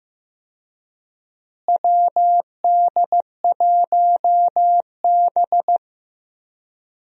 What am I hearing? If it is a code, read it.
Morse code, heard as a keyed tone, about 15 wpm, 700 Hz. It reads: WD1B
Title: USA Call Signs - 4 Characters - Repeated 2x - 15wpm